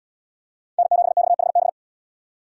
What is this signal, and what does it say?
Morse code, heard as a keyed tone, about 78 wpm, 700 Hz. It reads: N7XHB